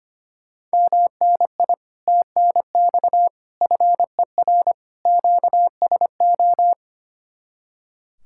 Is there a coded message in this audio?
Yes: Morse code from 0.7 to 6.7 s, heard as a keyed tone, about 32 wpm, 700 Hz. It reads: MNI TNX FER QSO